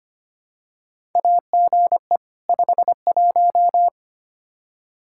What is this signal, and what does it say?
Morse code, heard as a keyed tone, about 25 wpm, 700 Hz. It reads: AGE 51
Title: Practice QSO - Exchange 3 - More Info - 25wpm